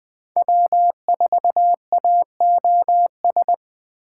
Morse code, heard as a keyed tone, about 21 wpm, 700 Hz. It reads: W4AOS
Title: Speed Racing - USA 1x3 Call Signs 17wpm